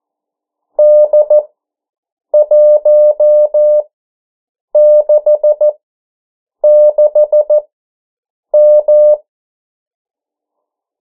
Morse code, heard as a keyed tone, about 14 wpm, spaced as for 7 wpm, 600 Hz. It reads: D166M
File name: morse mp3 / 09 urok 8